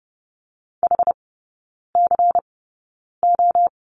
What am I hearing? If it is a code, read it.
Morse code, heard as a keyed tone, about 30 wpm, spaced as for 9 wpm, 700 Hz. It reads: HCO
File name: Lesson 16e - Letter c - Three Characters 30wpm